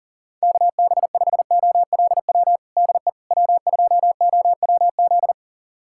Morse code, heard as a keyed tone, about 40 wpm, 700 Hz. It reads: KB5OLW DE W2OWZ
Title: USA Callsign DE Callsign 40wpm